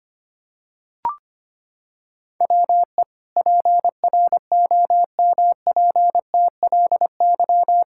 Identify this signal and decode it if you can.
Morse code, heard as a keyed tone, about 25 wpm, 700 Hz. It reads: WE PROMPTLY
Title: Pangram Challenge v2 - 25wpm